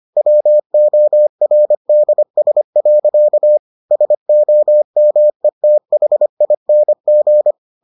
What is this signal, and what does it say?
Morse code, heard as a keyed tone, about 25 wpm, 600 Hz. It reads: WORDS. SOMETHING